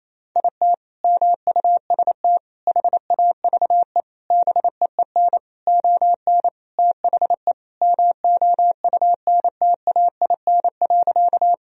Morse code, heard as a keyed tone, about 28 wpm, 700 Hz. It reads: IT MUST HAVE BEEN ON THE MOUNTAIN.
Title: Sentences from Top 1000 Words - Review Part 1 - 28wpm